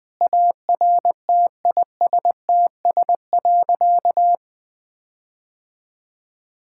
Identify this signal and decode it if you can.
Morse code, heard as a keyed tone, about 20 wpm, 700 Hz. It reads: ARTISTS.
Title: Sentences with 8-Word Phrases - Spoken Beginning - 20wpm